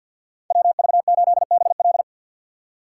Morse code, heard as a keyed tone, about 51 wpm, 700 Hz. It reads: WV8BL
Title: USA Call Signs - 5 Characters - 3x Character Spacing - 50wpm